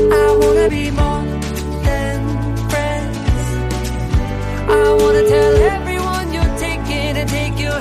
Ringback tone (Europe, 425 Hz), with noise.